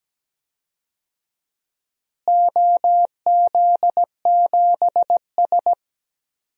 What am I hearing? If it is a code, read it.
Morse code, heard as a keyed tone, about 17 wpm, 700 Hz. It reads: OZ7S
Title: DXCC-100 Call Signs - 4 Characters - Repeated 2x - 17wpm